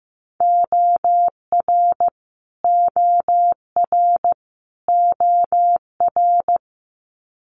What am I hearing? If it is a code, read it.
Morse code, heard as a keyed tone, about 15 wpm, 700 Hz. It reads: OR OR OR
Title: Multi-letter Phonemes Repeated 3x 15wpm